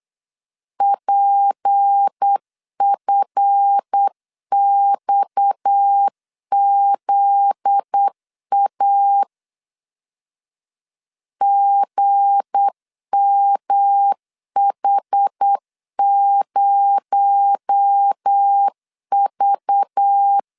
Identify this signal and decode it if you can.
Morse code, heard as a keyed tone, about 8 wpm, 800 Hz. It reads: PFXZA GMH0V
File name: LEZIONI / 7 GF9Z / 7 GF9Z 8wpm 3213